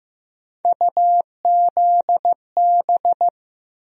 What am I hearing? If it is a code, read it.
Morse code, heard as a keyed tone, about 15 wpm, 700 Hz. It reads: UZB
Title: USA Call Sign Postfixes - 3 Letters 15wpm